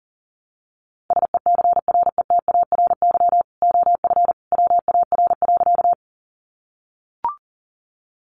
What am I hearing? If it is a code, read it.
Morse code, heard as a keyed tone, about 40 wpm, 700 Hz. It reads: SECRETARY OF WAR.
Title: Sentences from Top 1300 Words 40wpm